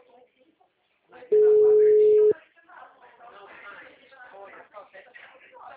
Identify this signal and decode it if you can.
Ringback tone (Europe, 425 Hz), with speech.
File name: out- -2023